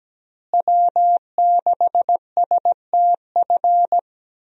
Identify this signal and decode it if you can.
Morse code, heard as a keyed tone, about 17 wpm, 700 Hz. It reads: W6STF
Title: USA Call Signs - 5 Characters - Repeated 2x - 17wpm